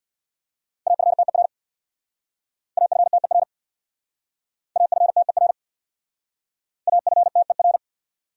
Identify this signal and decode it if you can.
Morse code, heard as a keyed tone, about 63 wpm, 700 Hz. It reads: AFTER AFTER AFTER AFTER